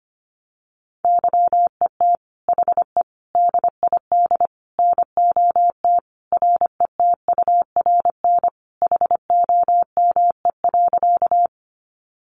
Morse code, heard as a keyed tone, about 25 wpm, 700 Hz. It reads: YET HE DID NOT RETURN HOME.
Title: Sentences from Top 800 Words 25wpm